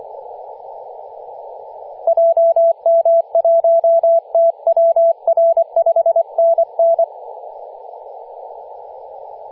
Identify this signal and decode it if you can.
Morse code, heard as a keyed tone, about 24 wpm, 670 Hz, with radio noise. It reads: JM1TWR 5NN
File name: posts / 2016-nov-10-0528z-t30ma-21-007mhz-cw